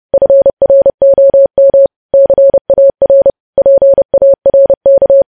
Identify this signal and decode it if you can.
Morse code, heard as a keyed tone, about 30 wpm, 550 Hz. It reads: FROM CAR PARK
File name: podcast / News headlines-1x-30x30